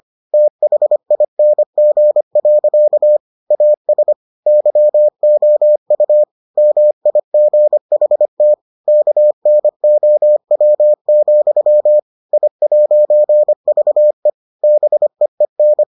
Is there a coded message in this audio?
Yes: Morse code from 0.3 to 15.8 s, heard as a keyed tone, about 25 wpm, 600 Hz. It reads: THING. AS YOU MIGHT KNOW, I'VE BEEN